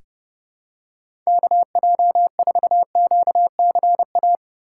Morse code, heard as a keyed tone, about 30 wpm, 700 Hz. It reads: KJ4QCA
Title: USA Call Signs - 6 Characters - 4x Character Spacing - 30wpm